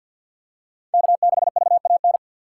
Morse code, heard as a keyed tone, about 53 wpm, 700 Hz. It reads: K6VAN
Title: USA Call Signs - 5 Characters - 6x Character Spacing - 50wpm